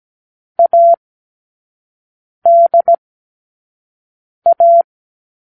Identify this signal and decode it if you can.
Morse code, heard as a keyed tone, about 17 wpm, 680 Hz. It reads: A D A
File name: k7qo / Lesson6(A-D)